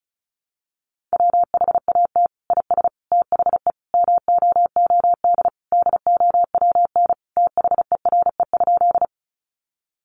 Morse code, heard as a keyed tone, about 34 wpm, 700 Hz. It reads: WHAT IS THE MOOD DOWN THERE?
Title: Sentences from Top 1100 Words 35wpm